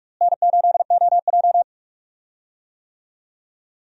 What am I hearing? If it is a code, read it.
Morse code, heard as a keyed tone, about 46 wpm, 700 Hz. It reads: N8OJ
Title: USA Call Signs - 4 Characters - Repeated 2x - 45wpm